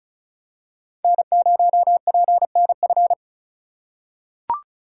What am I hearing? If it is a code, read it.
Morse code, heard as a keyed tone, about 36 wpm, 700 Hz. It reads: N0PNF
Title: USA Call Signs - 5 Characters - 3x Character Spacing - 35wpm